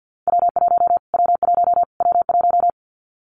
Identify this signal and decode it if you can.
Morse code, heard as a keyed tone, about 50 wpm, 700 Hz. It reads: W1 W1 W1
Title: Callsign Prefixes for Top 25 DXCC - Repeated 3x - 50wpm